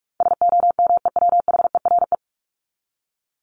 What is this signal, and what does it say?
Morse code, heard as a keyed tone, about 44 wpm, 700 Hz. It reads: SOMEWHERE
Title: Top 1251-1500 Words 45wpm